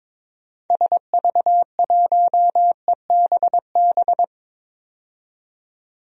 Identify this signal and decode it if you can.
Morse code, heard as a keyed tone, about 22 wpm, 700 Hz. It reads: SV1EBB
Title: DXCC-100 Call Signs - 6 Characters - 22wpm